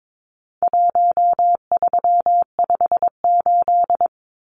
Morse code, heard as a keyed tone, about 22 wpm, 700 Hz. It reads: 1358
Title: Numbers - Contest Style - 4-Digit 22wpm